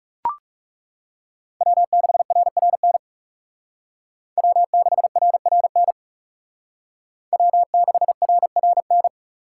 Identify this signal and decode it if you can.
Morse code, heard as a keyed tone, about 36 wpm, 700 Hz. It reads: W6RRN W6RRN W6RRN
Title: Speed Racing - USA 1x3 Call Signs 30wpm